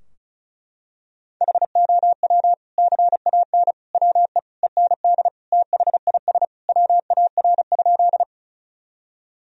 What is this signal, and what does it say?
Morse code, heard as a keyed tone, about 36 wpm, 700 Hz. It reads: HOW CAN WE END THIS WAR?
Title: Sentences from Top 1000 Words - Review Part 2 - 35wpm